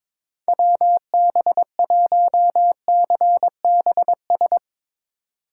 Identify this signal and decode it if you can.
Morse code, heard as a keyed tone, about 24 wpm, 700 Hz. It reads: WB1CBS
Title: Speed Racing - USA Call Signs 22wpm